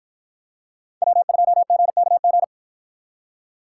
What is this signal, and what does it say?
Morse code, heard as a keyed tone, about 50 wpm, 700 Hz. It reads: W2GKG